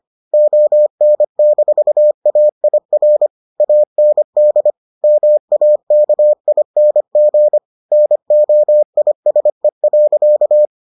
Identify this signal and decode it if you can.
Morse code, heard as a keyed tone, about 25 wpm, 600 Hz. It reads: ON-AIR AND MAKING NOISE.